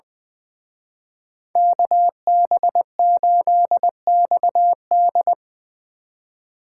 Morse code, heard as a keyed tone, about 23 wpm, 700 Hz. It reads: KB8XD
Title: Speed Racing - USA Call Signs 20wpm